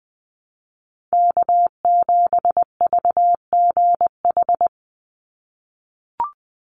Morse code, heard as a keyed tone, about 20 wpm, 700 Hz. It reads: K7VGH
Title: USA Call Signs 20wpm